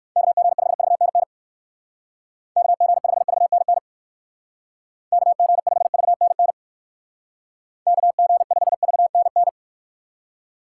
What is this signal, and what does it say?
Morse code, heard as a keyed tone, about 59 wpm, 700 Hz. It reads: KG5VNN KG5VNN KG5VNN KG5VNN